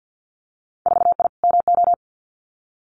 Morse code, heard as a keyed tone, about 50 wpm, 700 Hz. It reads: 4I MO
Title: Winter Field Day Exchanges 50wpm